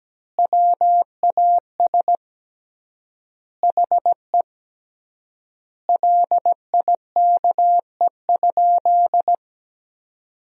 Morse code, heard as a keyed tone, about 17 wpm, 700 Hz. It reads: WAS HE LIKE?
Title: 3x Word-Spacing - Sentences from Top 100 Words - 17wpm